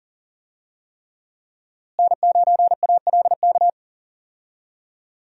Morse code, heard as a keyed tone, about 40 wpm, 700 Hz. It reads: N9ALK
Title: USA Call Signs - 5 Characters - Repeated 3x - 40wpm